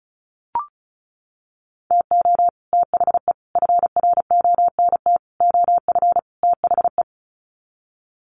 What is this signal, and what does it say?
Morse code, heard as a keyed tone, about 34 wpm, 700 Hz. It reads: TO THE FRONT OF THE
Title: Sets of 5 Words 35wpm